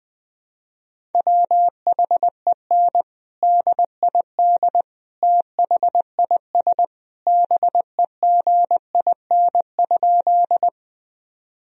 Morse code, heard as a keyed tone, about 21 wpm, 700 Hz. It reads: WHEN DID THIS BEGIN?
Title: Sentences from Top 1000 Words - Review Part 1 - 20wpm